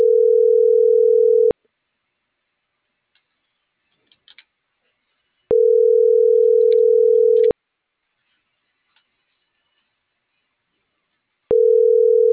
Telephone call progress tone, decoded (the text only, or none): ringback tone